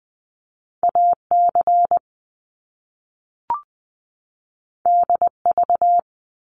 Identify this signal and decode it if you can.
Morse code, heard as a keyed tone, about 20 wpm, 700 Hz. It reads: AC DV
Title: Top Two Letter Combinations in Words 20wpm